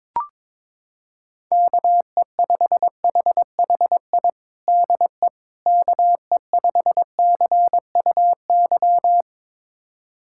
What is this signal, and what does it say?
Morse code, heard as a keyed tone, about 22 wpm, 700 Hz. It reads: KE5HHI DE KE5CUY
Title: USA Callsign DE Callsign - 6 Characters - 22wpm